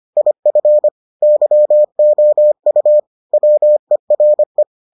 Morse code, heard as a keyed tone, about 25 wpm, 600 Hz. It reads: IF YOU WERE